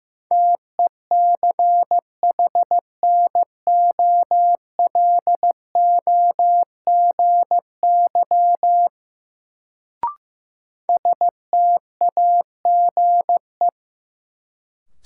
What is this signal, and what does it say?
Morse code, heard as a keyed tone, about 15 wpm, 700 Hz. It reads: TECHNOLOGY STAGE
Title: Sentences from Top 1200 Words - New Word Drill Alt - 15wpm